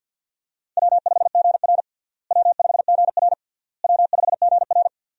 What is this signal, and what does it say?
Morse code, heard as a keyed tone, about 52 wpm, 700 Hz. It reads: W5GR W5GR W5GR
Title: USA Call Signs - 4 Characters - Repeated 3x - 50wpm